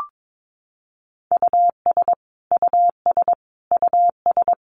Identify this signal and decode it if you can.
Morse code, heard as a keyed tone, about 22 wpm, 700 Hz. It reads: US US US
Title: Top 200 Words Repeated 22wpm